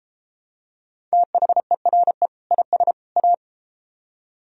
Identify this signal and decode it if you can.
Morse code, heard as a keyed tone, about 31 wpm, 700 Hz. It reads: THERE IS A